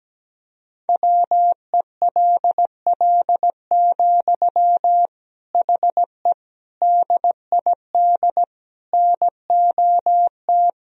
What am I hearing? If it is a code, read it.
Morse code, heard as a keyed tone, about 17 wpm, 700 Hz. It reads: WELL, HE DID NOT
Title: Sentences from Top 1000 Words - Review Part 2 - 17wpm